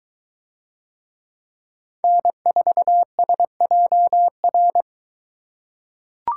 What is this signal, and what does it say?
Morse code, heard as a keyed tone, about 23 wpm, 700 Hz. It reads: N4SJR